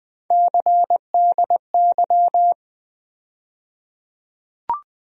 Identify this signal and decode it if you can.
Morse code, heard as a keyed tone, about 20 wpm, 700 Hz. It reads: CDY
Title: USA Call Sign Postfixes - 3 Letters 20wpm